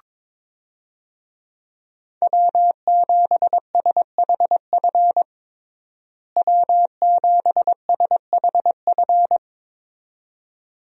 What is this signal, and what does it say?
Morse code, heard as a keyed tone, about 22 wpm, 700 Hz. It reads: W7SHF W7SHF
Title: USA Call Signs - 5 Characters - Repeated 2x - 22wpm